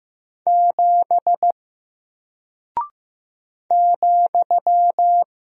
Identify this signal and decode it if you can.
Morse code, heard as a keyed tone, about 15 wpm, 700 Hz. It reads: 7 ,
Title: Single Character (Letter:Number:Punctuation:Prosign) Alt 15wpm